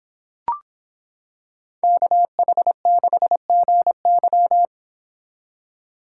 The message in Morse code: KH6GY

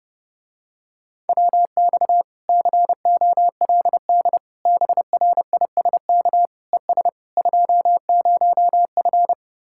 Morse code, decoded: WX COLD BRISK ES 20F

